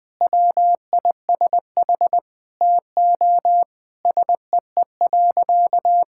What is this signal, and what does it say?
Morse code, heard as a keyed tone, about 21 wpm, 700 Hz. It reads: WISH TO SEE.
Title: Sentences with 8-Word Phrases - 20wpm